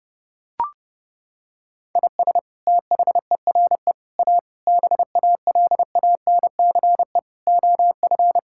Morse code, heard as a keyed tone, about 32 wpm, 700 Hz. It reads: IS THERE A BALANCE OF
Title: Sentences from Top 1000 Words - Review Part 6 - 30wpm